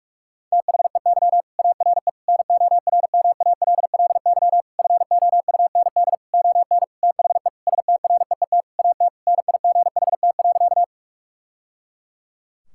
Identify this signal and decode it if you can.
Morse code, heard as a keyed tone, about 41 wpm, 700 Hz. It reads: THEY ARE NORMALLY FOUND ON THE STREET AT NIGHT.
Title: Sentences from Top 1000 Words - Review Part 6 - 45wpm